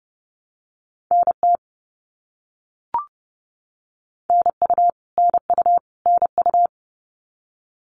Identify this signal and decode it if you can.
Morse code, heard as a keyed tone, about 31 wpm, 700 Hz. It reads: NT NU NU NU
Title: Canadian Provinces and Territories - Repeated 3x - 30wpm